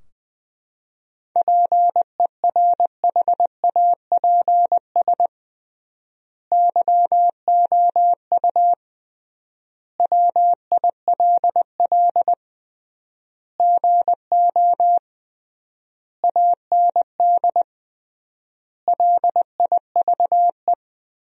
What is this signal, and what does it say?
Morse code, heard as a keyed tone, about 21 wpm, 700 Hz. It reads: PERHAPS YOU WILL GO AND LIVE
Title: 3x Word-Spacing - Sentences from Top 500 Words - 20wpm